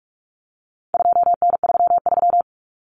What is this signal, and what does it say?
Morse code, heard as a keyed tone, about 44 wpm, 700 Hz. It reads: 2N33
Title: Common Cut Numbers - 4-Digit 45wpm